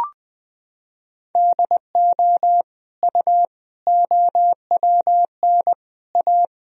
Morse code, heard as a keyed tone, about 20 wpm, 700 Hz. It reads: DO U OWN A
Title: Sentences from Top 400 Words and Abbreviations 20wpm